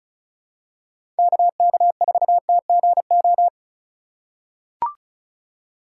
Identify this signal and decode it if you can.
Morse code, heard as a keyed tone, about 36 wpm, 700 Hz. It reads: KK4TGO
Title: USA Call Signs - 6 Characters - 4x Character Spacing - 35wpm